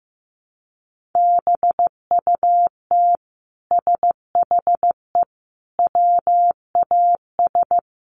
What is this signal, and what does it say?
Morse code, heard as a keyed tone, about 15 wpm, 700 Hz. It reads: BUT SHE WAS
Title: Sentences from Top 1500 Words 15wpm